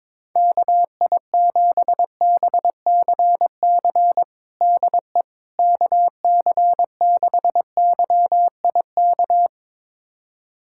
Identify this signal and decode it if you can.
Morse code, heard as a keyed tone, about 22 wpm, 700 Hz. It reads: KI7BCC DE KC6YIK
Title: USA Callsign DE Callsign 22wpm